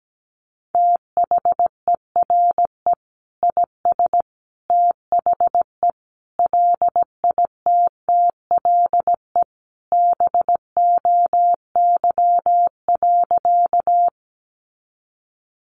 Morse code, heard as a keyed tone, about 17 wpm, 700 Hz. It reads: THERE IS THE LITTLE BOY.
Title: Sentences from Top 200 Words 17wpm